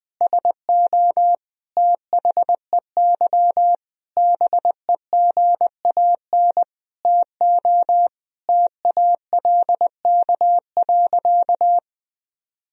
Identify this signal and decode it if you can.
Morse code, heard as a keyed tone, about 20 wpm, 700 Hz. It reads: SO THEY BEGAN TO TALK.
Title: Sentences from Top 400 Words and Abbreviations 20wpm